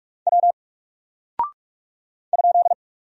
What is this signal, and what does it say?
Morse code, heard as a keyed tone, about 44 wpm, 700 Hz. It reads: W ?
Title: Single Character (Letter:Number:Punctuation:Prosign) 45wpm